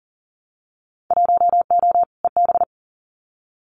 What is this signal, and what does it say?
Morse code, heard as a keyed tone, about 40 wpm, 700 Hz. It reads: 1O EB
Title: Winter Field Day Exchanges 40wpm